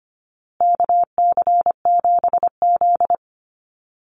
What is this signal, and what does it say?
Morse code, heard as a keyed tone, about 25 wpm, 700 Hz. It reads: KC7Z
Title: USA Call Signs Repeated 3x 25wpm